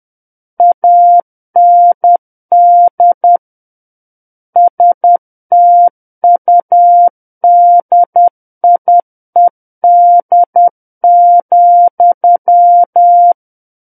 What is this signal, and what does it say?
Morse code, heard as a keyed tone, about 10 wpm, 700 Hz. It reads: AND STUDIED,
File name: Book One / 01-Chapter 1 - 10 WPM